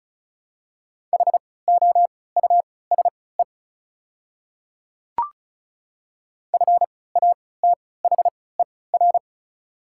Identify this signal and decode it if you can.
Morse code, heard as a keyed tone, about 35 wpm, spaced as for 20 wpm, 690 Hz. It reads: HOUSE FATHER
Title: 3x Character Spacing - Top 500 Words - 35wpm